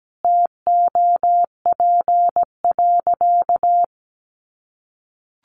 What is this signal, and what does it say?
Morse code, heard as a keyed tone, about 17 wpm, 700 Hz. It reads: TOP.
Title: Sentences from Top 1300 Words 17wpm